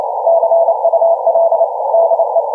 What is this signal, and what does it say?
Morse code, heard as a keyed tone, about 28 wpm, 630 Hz, with noise. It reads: CUL DE